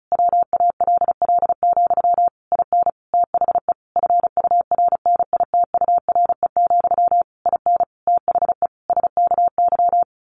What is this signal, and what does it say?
Morse code, heard as a keyed tone, about 34 wpm, 700 Hz. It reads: WALL, IN THE FURNITURE, IN THE SKY
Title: A Sound Of Thunder 35wpm